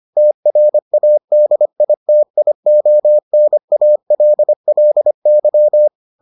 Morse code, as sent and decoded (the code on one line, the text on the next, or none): - .-. .- -.. .. - .. --- -. .- .-.. .-.. -.--
TRADITIONALLY